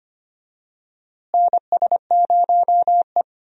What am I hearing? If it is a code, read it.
Morse code, heard as a keyed tone, about 25 wpm, 700 Hz. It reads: NS0E
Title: USA Call Signs - 4 Characters - Repeated 2x - 25wpm